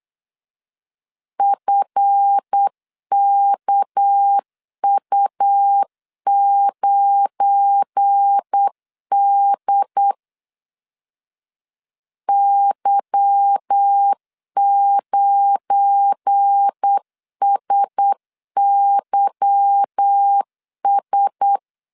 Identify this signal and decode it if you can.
Morse code, heard as a keyed tone, about 8 wpm, 800 Hz. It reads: FKU9D Y9SYS
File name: LEZIONI / 8 YJ45 / 8 Y 8wpm 3245